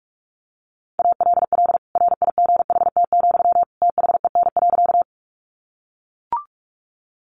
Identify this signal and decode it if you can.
Morse code, heard as a keyed tone, about 44 wpm, 700 Hz. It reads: ALL RIGHT, THEN.
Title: Sentences from Top 200 Words 45wpm